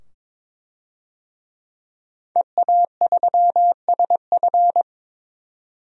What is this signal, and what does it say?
Morse code, heard as a keyed tone, about 22 wpm, 700 Hz. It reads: EA3SF
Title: DXCC-100 Call Signs - 5 Characters - 22wpm